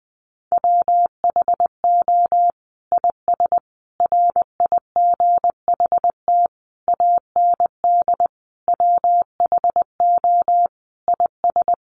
Morse code, heard as a keyed tone, about 20 wpm, 700 Hz. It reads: WHO IS RIGHT AND WHO IS
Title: Sentences from Top 900 Words 20wpm